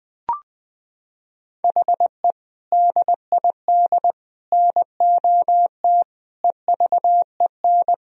Morse code, heard as a keyed tone, about 20 wpm, 700 Hz. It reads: HE DID NOT EVEN
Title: Sentences from Top 1000 Words - Review Part 1 - 20wpm